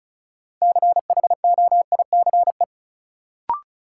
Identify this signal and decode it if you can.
Morse code, heard as a keyed tone, about 36 wpm, 700 Hz. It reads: CHOICE
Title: Sets of 5 Words - New Word Drill - 35wpm